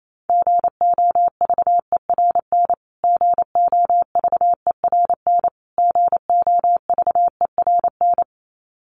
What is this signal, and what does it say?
Morse code, heard as a keyed tone, about 28 wpm, 700 Hz. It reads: GOVERN GOVERN GOVERN
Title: Top 500 Words Repeated 28wpm